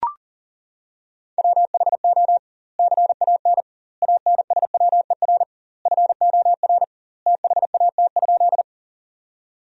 Morse code, heard as a keyed tone, about 40 wpm, 700 Hz. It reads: WHO CAN ANSWER FOR THAT?
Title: 2x Word-Spacing - Sentences from Top 300 Words - 40wpm